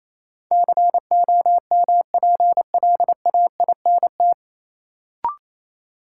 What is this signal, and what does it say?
Morse code, heard as a keyed tone, about 28 wpm, 700 Hz. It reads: COMPLAINT
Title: Sentences from Top 1400 Words - New Word Drill Alt - 28wpm